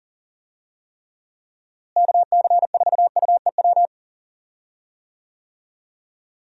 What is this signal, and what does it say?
Morse code, heard as a keyed tone, about 38 wpm, 700 Hz. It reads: KC4UEW
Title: USA Call Signs - 6 Characters - Repeated 2x - 40wpm